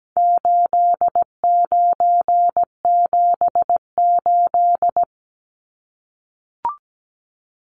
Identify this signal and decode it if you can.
Morse code, heard as a keyed tone, about 17 wpm, 700 Hz. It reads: 8978
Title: Numbers - 4-Digit 17wpm